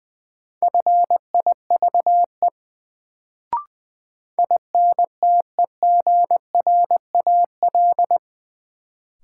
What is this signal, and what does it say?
Morse code, heard as a keyed tone, about 20 wpm, 700 Hz. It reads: FIVE INTEGRAL
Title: Sets of 5 Words - New Word Drill - 20wpm